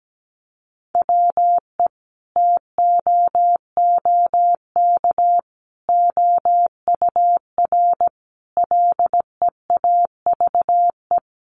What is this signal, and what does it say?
Morse code, heard as a keyed tone, about 17 wpm, 700 Hz. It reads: WE TOOK OUR LEAVE
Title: Sentences from Top 400 Words 17wpm